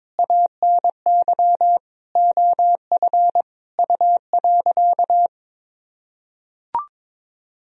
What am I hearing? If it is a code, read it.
Morse code, heard as a keyed tone, about 22 wpm, 700 Hz. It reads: ANY OF U.
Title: Sentences from Top 400 Words and Abbreviations 22wpm